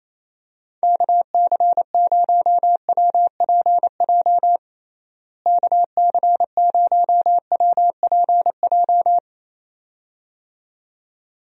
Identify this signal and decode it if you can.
Morse code, heard as a keyed tone, about 28 wpm, 700 Hz. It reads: KC0WPJ KC0WPJ
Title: USA Call Signs - 6 Characters - Repeated 2x - 28wpm